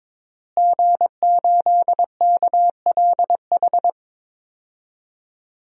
Morse code, heard as a keyed tone, about 22 wpm, 700 Hz. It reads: G8KLH
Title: DXCC-100 Call Signs - 5 Characters - Repeated 3x - 22wpm